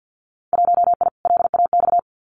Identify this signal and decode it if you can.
Morse code, heard as a keyed tone, about 50 wpm, 700 Hz. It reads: 1I LAX
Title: Winter Field Day Exchanges Encore 50wpm